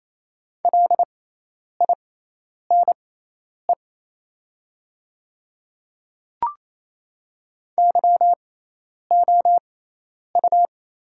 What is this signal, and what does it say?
Morse code, heard as a keyed tone, about 28 wpm, spaced as for 10 wpm, 690 Hz. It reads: LINE YOU
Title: 6x Character Spacing - Top 200 Words - 28wpm